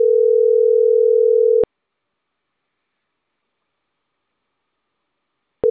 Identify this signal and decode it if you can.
Ringback tone (North America), 440 + 480 Hz.